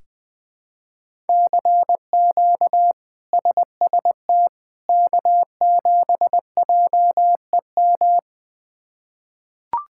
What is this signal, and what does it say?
Morse code, heard as a keyed tone, about 20 wpm, 700 Hz. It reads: CQ SST K7JEM
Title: K1USN SST QSO Practice - 20wpm